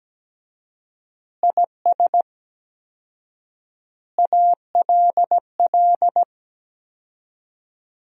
Morse code, heard as a keyed tone, about 17 wpm, 700 Hz. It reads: IS ALL
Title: 4x Word-Spacing - Sentences from Top 300 Words - 17wpm